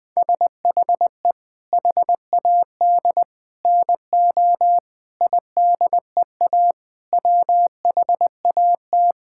Morse code, heard as a keyed tone, about 20 wpm, 700 Hz. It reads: SHE HAD NO IDEA WHAT
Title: Sentences with 7-Word Phrases - 20wpm